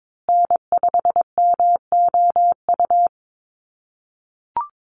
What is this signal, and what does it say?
Morse code, heard as a keyed tone, about 22 wpm, 700 Hz. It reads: N5MOU
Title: USA Call Signs Repeated 3x 22wpm